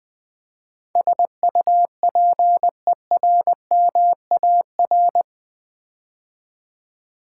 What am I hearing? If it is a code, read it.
Morse code, heard as a keyed tone, about 20 wpm, 700 Hz. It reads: SUPERMAR
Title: Build A Word - 4 Letters At A Time - 11-Letter Words - 20wpm